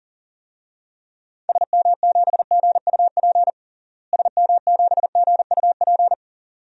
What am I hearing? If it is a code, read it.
Morse code, heard as a keyed tone, about 40 wpm, 700 Hz. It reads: SM7GUP SM7GUP
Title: DXCC-100 Call Signs - 6 Characters - Repeated 2x - 40wpm